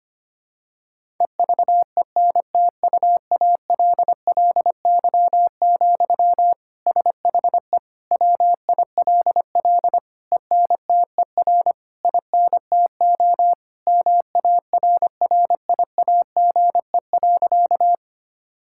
Morse code, heard as a keyed tone, about 25 wpm, 700 Hz. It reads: EVENTUALLY, SHE WILL ENTER INTO MARRIAGE.
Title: Sentences from Top 1000 Words - Review Part 6 - 25wpm